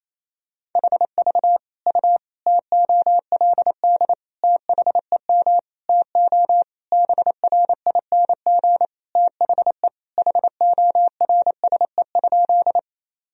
Morse code, heard as a keyed tone, about 27 wpm, 700 Hz. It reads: HV U TOLD THEM TO BRING THE HORSE?
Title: Sentences from Top 500 Words and Abbreviations 28wpm